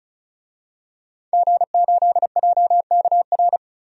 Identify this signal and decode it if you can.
Morse code, heard as a keyed tone, about 34 wpm, 700 Hz. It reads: G8JKR
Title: DXCC-100 Call Signs - 5 Characters - 35wpm